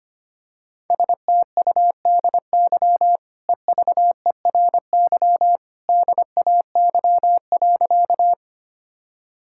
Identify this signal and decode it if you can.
Morse code, heard as a keyed tone, about 25 wpm, 690 Hz. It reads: STUDY EVERY DAY.
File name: Easy Sentences with Hints - 25wpm